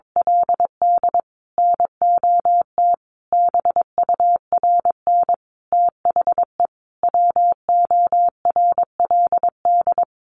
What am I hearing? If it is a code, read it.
Morse code, heard as a keyed tone, about 22 wpm, 700 Hz. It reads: LD NOT BURN THE WORLD
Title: Sentences from Top 1300 Words 22wpm